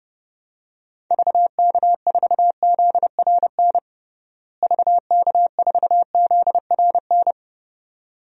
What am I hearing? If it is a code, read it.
Morse code, heard as a keyed tone, about 30 wpm, 700 Hz. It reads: VK4ZRN VK4ZRN
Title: DXCC-100 Call Signs - 6 Characters - Repeated 2x - 30wpm